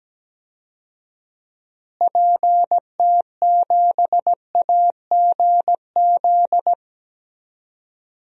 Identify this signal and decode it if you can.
Morse code, heard as a keyed tone, about 17 wpm, 700 Hz. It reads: PT7AGZ
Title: DXCC-100 Call Signs - 6 Characters - Repeated 3x - 17wpm